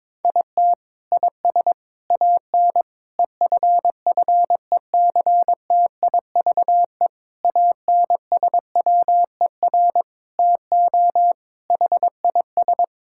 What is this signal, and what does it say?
Morse code, heard as a keyed tone, about 22 wpm, 700 Hz. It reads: IT IS AN EFFECTIVE ANSWER TO HIS
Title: Sentences from Top 1000 Words - Review Part 6 - 22wpm